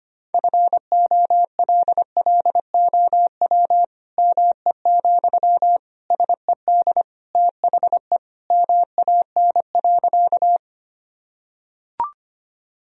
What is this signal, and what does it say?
Morse code, heard as a keyed tone, about 25 wpm, 700 Hz. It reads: FOLLOW ME, SED THE MAN.
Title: Sentences from Top 200 Words and Abbreviations 25wpm